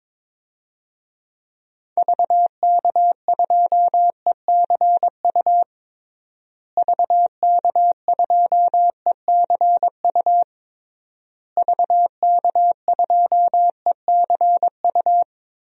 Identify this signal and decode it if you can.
Morse code, heard as a keyed tone, about 22 wpm, 700 Hz. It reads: VK2ECU VK2ECU VK2ECU
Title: DXCC-100 Call Signs - 6 Characters - Repeated 3x - 22wpm